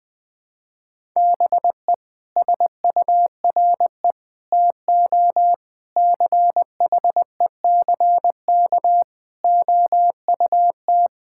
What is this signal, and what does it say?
Morse code, heard as a keyed tone, about 20 wpm, 700 Hz. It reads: BE SURE TO CHECK OUT
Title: Sets of 6 Words - 20wpm